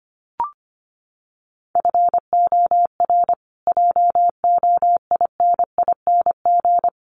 Morse code, heard as a keyed tone, about 25 wpm, 700 Hz. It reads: FOR JOINING
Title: Sets of 5 Words Encore 25wpm